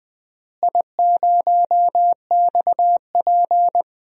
Morse code, heard as a keyed tone, about 21 wpm, 700 Hz. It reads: I0XP
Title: DXCC-100 Call Signs - 4 Characters - Repeated 2x - 20wpm